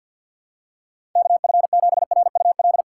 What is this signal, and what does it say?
Morse code, heard as a keyed tone, about 52 wpm, 700 Hz. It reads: KF7RUL
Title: USA Call Signs - 6 Characters - 3x Character Spacing - 50wpm